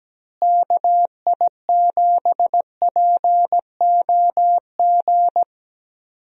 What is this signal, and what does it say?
Morse code, heard as a keyed tone, about 17 wpm, 700 Hz. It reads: KI7POG
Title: USA Call Signs - 6 Characters - Repeated 3x - 17wpm